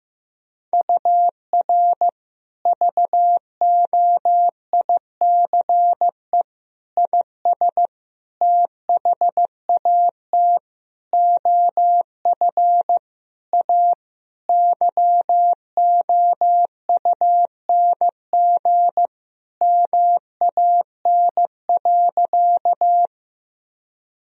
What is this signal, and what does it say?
Morse code, heard as a keyed tone, about 15 wpm, 700 Hz. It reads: UR VOICE IS THAT OF A YOUNG MAN.
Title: Sentences from Top 500 Words and Abbreviations 15wpm